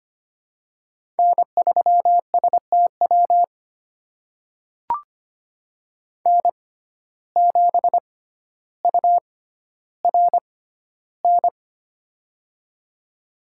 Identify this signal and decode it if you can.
Morse code, heard as a keyed tone, about 25 wpm, spaced as for 13 wpm, 700 Hz. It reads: N3STW N7URN
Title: USA Call Signs - 6x Character Spacing - 25wpm